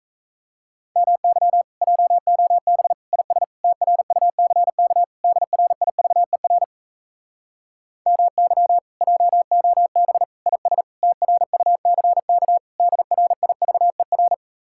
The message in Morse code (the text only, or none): MY JOB IS TRUCK DRIVER MY JOB IS TRUCK DRIVER